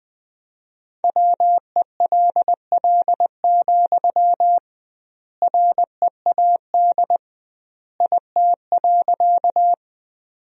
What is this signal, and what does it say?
Morse code, heard as a keyed tone, about 20 wpm, 700 Hz. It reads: WELL, READ IT.
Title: 2x Word-Spacing - Sentences from Top 200 Words - 20wpm